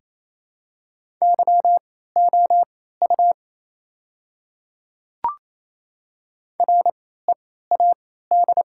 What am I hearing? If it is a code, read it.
Morse code, heard as a keyed tone, about 28 wpm, spaced as for 16 wpm, 700 Hz. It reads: YOU READ
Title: 3x Character Spacing - Top 200 Words - 28wpm